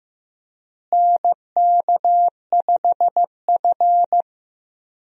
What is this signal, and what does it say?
Morse code, heard as a keyed tone, about 15 wpm, 700 Hz. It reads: NK5F
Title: USA Call Signs - 4 Characters - Repeated 3x - 15wpm